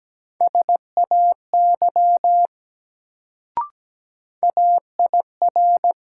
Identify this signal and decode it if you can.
Morse code, heard as a keyed tone, about 17 wpm, 700 Hz. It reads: SAY AIR
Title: Sets of 2 Words - New Word Drill - 17wpm